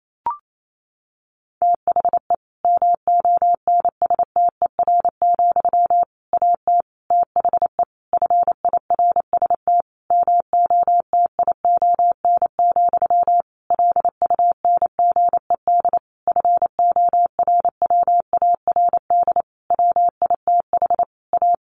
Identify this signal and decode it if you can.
Morse code, heard as a keyed tone, about 28 wpm, 700 Hz. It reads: THE MONSTER, AT THE FIRST MOTION, LUNGED FORWARD WITH A
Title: A Sound Of Thunder 28wpm